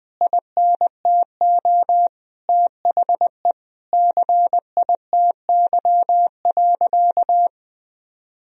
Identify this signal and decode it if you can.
Morse code, heard as a keyed tone, about 20 wpm, 700 Hz. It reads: INTO THE CITY.
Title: Sentences from Top 1000 Words - Review Part 1 - 20wpm